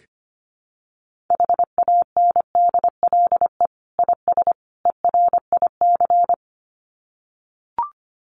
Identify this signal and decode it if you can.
Morse code, heard as a keyed tone, about 25 wpm, 700 Hz. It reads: HANDLE IS ERIC
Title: Common QSO Phrases 25wpm